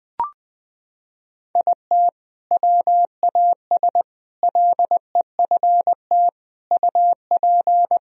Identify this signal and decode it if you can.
Morse code, heard as a keyed tone, about 20 wpm, 700 Hz. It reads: IT WAS LEFT UP
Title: Sentences from Top 1000 Words - Review Part 4 - 20wpm